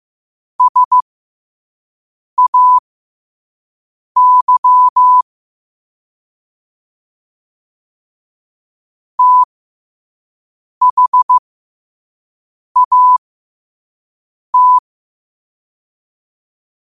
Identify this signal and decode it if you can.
Morse code, heard as a keyed tone, about 15 wpm, spaced as for 5 wpm, 1000 Hz. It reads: SAY THAT
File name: qotdfiles / 5wpm